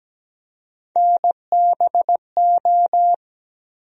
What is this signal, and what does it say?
Morse code, heard as a keyed tone, about 17 wpm, 700 Hz. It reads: NBO
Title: USA Call Sign Postfixes - 3 Letters 17wpm